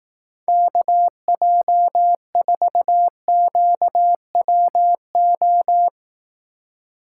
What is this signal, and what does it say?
Morse code, heard as a keyed tone, about 18 wpm, 700 Hz. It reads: KJ4QWO
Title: Speed Racing - USA Call Signs 15wpm